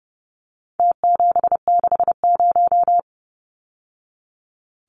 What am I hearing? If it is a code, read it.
Morse code, heard as a keyed tone, about 30 wpm, 700 Hz. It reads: T760
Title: Numbers - Contest Style - 4-Digit 30wpm